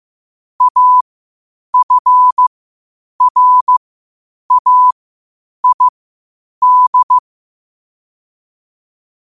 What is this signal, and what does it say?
Morse code, heard as a keyed tone, about 15 wpm, spaced as for 8 wpm, 1000 Hz. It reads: AFRAID
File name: qotdfiles / 8wpm